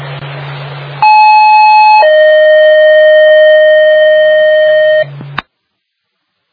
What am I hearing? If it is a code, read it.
Two-tone page, heard as a one-second tone, then a three-second tone, about 860 then 620 Hz. It is heard from 1.0 to 5.0 s.